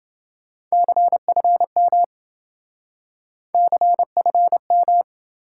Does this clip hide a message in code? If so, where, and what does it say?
Yes: Morse code from 0.7 to 5.0 s, heard as a keyed tone, about 27 wpm, 700 Hz. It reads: CFM CFM